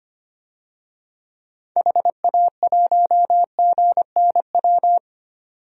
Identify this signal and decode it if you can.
Morse code, heard as a keyed tone, about 25 wpm, 700 Hz. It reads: HA1GNW
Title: DXCC-100 Call Signs - 6 Characters - 25wpm